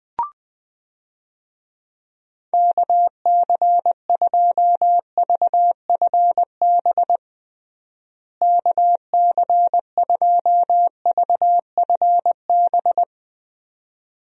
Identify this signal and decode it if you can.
Morse code, heard as a keyed tone, about 20 wpm, 700 Hz. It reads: KC2VFB KC2VFB
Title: USA Call Signs - 6 Characters - Repeated 2x - 20wpm